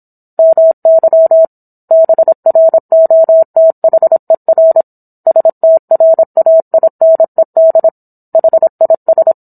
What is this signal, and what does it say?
Morse code, heard as a keyed tone, about 27 wpm, 650 Hz. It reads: MY BROTHER STRAINED HIS